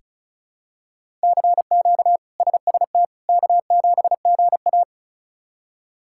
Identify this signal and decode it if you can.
Morse code, heard as a keyed tone, about 34 wpm, 700 Hz. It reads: CQ SST K7GA
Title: K1USN SST QSO Practice - 35wpm